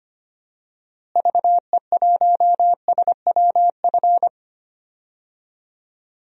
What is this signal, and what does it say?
Morse code, heard as a keyed tone, about 25 wpm, 700 Hz. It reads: VE1SWF
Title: DXCC-100 Call Signs - 6 Characters - Repeated 2x - 25wpm